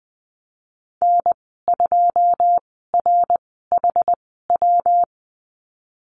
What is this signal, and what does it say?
Morse code, heard as a keyed tone, about 20 wpm, spaced as for 15 wpm, 700 Hz. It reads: N2RHW
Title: 128 Lesson 21g - Number 6 - Call Signs 20wpm